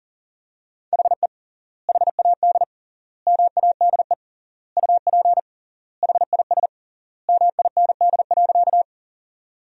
Morse code, heard as a keyed tone, about 40 wpm, 700 Hz. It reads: HE HAD MADE UP HIS MIND.
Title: 3x Word-Spacing - Sentences from Top 500 Words - 40wpm